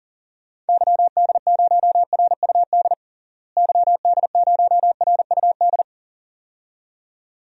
Morse code, heard as a keyed tone, about 40 wpm, 700 Hz. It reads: YD0RUD YD0RUD
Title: DXCC-100 Call Signs - 6 Characters - Repeated 2x - 40wpm